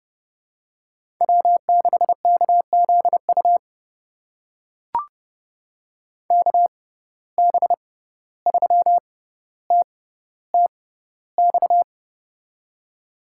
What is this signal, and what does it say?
Morse code, heard as a keyed tone, about 30 wpm, spaced as for 15 wpm, 700 Hz. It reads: W6KZU KB3TTX
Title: USA Call Signs - 6x Character Spacing - 30wpm